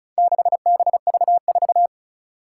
Morse code, heard as a keyed tone, about 36 wpm, 700 Hz. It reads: 6BV4
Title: Four Challenging Letter-Number Combinations 35wpm